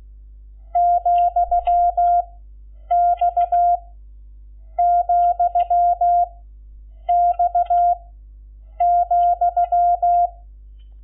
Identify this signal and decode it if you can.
Morse code, heard as a keyed tone, about 16 wpm, 700 Hz. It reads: , X , X ,
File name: Lecciones / CLASE Nº11